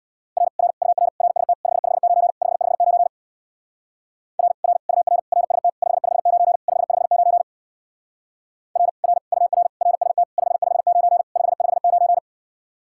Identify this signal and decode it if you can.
Morse code, heard as a keyed tone, about 63 wpm, 700 Hz. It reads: R R UR RST 559 559 R R UR RST 559 559 R R UR RST 559 559